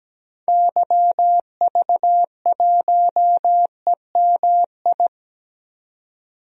Morse code, heard as a keyed tone, about 17 wpm, 700 Hz. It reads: YV1EMI
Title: DXCC-100 Call Signs - 6 Characters - 17wpm